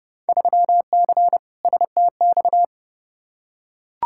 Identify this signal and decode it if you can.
Morse code, heard as a keyed tone, about 30 wpm, 700 Hz. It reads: 3C STX
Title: Field Day Exchanges Encore - 30wpm